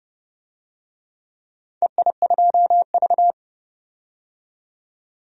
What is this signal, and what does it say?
Morse code, heard as a keyed tone, about 31 wpm, 700 Hz. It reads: EI2V
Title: DXCC-100 Call Signs - 4 Characters - 30wpm